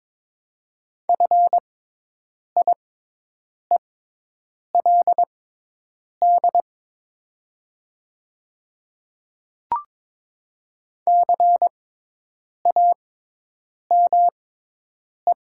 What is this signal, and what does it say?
Morse code, heard as a keyed tone, about 22 wpm, spaced as for 8 wpm, 700 Hz. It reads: FIELD CAME
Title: 6x Character Spacing - Top 500 Words - 22wpm